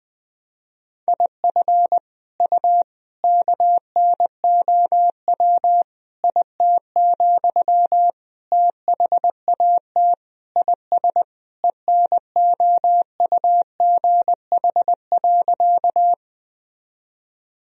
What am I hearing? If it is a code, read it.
Morse code, heard as a keyed tone, about 21 wpm, 700 Hz. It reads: IF U KNOW IT, THAT IS ENOUGH.
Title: Sentences from Top 400 Words and Abbreviations 20wpm